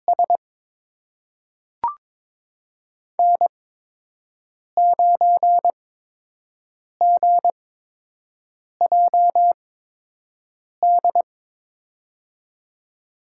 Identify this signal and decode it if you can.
Morse code, heard as a keyed tone, about 22 wpm, spaced as for 6 wpm, 700 Hz. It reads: S N9GJD
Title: USA Call Signs - 5 Characters - 8x Character Spacing - 22wpm